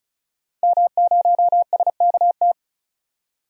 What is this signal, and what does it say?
Morse code, heard as a keyed tone, about 34 wpm, 700 Hz. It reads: M0SKT
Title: DXCC-100 Call Signs - 5 Characters - 35wpm